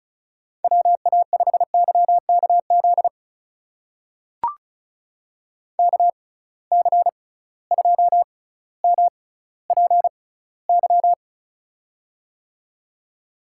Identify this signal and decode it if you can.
Morse code, heard as a keyed tone, about 36 wpm, spaced as for 18 wpm, 700 Hz. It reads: WA5YKZ KC2MPY
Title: USA Call Signs - 6 Characters - 6x Character Spacing - 35wpm